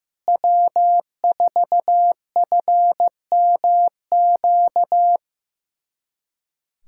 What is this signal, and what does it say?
Morse code, heard as a keyed tone, about 15 wpm, 700 Hz. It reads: W4FMQ
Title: USA Callsign DE Callsign - 5 Characters - 15wpm